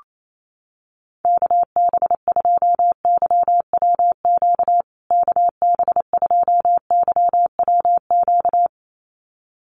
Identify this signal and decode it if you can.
Morse code, heard as a keyed tone, about 28 wpm, 700 Hz. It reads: KB2YWQ KB2YWQ
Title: USA Call Signs Repeated 2x 28wpm